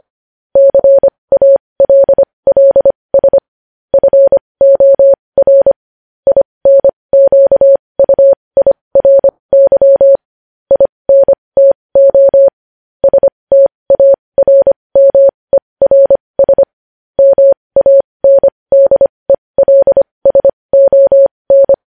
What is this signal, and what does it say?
Morse code, heard as a keyed tone, about 25 wpm, spaced as for 20 wpm, 550 Hz. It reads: CALLS FOR INQUIRY INTO STARMERS MANDELSON
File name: podcast / News headlines-1x-25x20